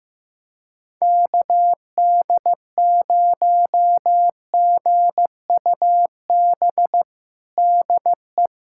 Morse code, heard as a keyed tone, about 15 wpm, 700 Hz. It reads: KD0GUB DE
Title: USA Callsign DE Callsign 15wpm